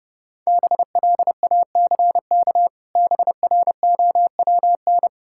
Morse code, heard as a keyed tone, about 32 wpm, 700 Hz. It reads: BLACK BROWN
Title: Pangram Challenge v2 - 30wpm